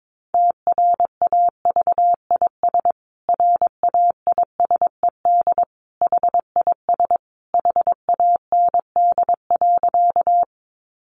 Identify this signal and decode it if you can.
Morse code, heard as a keyed tone, about 22 wpm, 700 Hz. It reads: TRAVIS RAISED HIS HAND.
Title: A Sound Of Thunder 22wpm